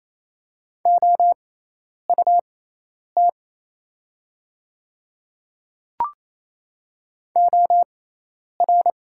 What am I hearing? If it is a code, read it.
Morse code, heard as a keyed tone, about 27 wpm, spaced as for 10 wpm, 700 Hz. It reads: OUT OR
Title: 6x Character Spacing - Top 100 Words - 28wpm